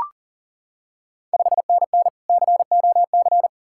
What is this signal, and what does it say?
Morse code, heard as a keyed tone, about 40 wpm, 690 Hz. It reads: 5NN COC
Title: Ontario QSO Party Exchange 40wpm